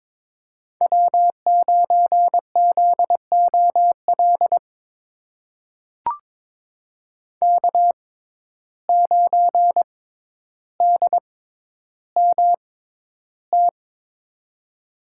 Morse code, heard as a keyed tone, about 22 wpm, spaced as for 11 wpm, 700 Hz. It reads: W9ZOL K9DMT
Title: USA Call Signs - 5 Characters - 6x Character Spacing - 22wpm